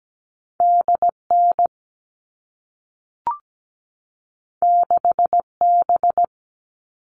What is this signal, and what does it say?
Morse code, heard as a keyed tone, about 17 wpm, 700 Hz. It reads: DN 6B
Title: All Cut Numbers - 2-Digit 17wpm